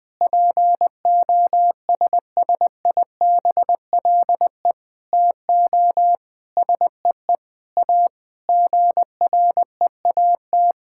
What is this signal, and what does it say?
Morse code, heard as a keyed tone, about 20 wpm, 700 Hz. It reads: POSSIBLE TO SEE A GREAT
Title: Sentences from Top 1600 Words 20wpm